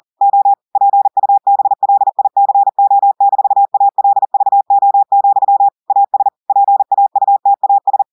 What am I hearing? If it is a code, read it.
Morse code, heard as a keyed tone, about 40 wpm, 800 Hz. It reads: O PUBLICO-ALVO, AS PAUTAS
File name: audios / portugais / portugais texte en clair 2 vitesse 40 mots minutes